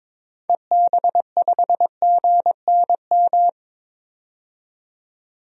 Morse code, heard as a keyed tone, about 22 wpm, 700 Hz. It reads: EB5GNM
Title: DXCC-100 Call Signs - 6 Characters - 22wpm